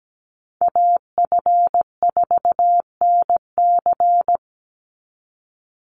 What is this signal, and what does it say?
Morse code, heard as a keyed tone, about 17 wpm, 700 Hz. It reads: AF4NC
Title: Call Signs Repeated 3x - 17wpm